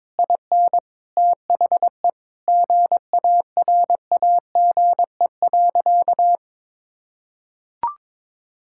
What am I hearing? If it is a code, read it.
Morse code, heard as a keyed tone, about 22 wpm, 700 Hz. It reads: IN THE GARAGE.
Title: Sentences from Top 1600 Words 22wpm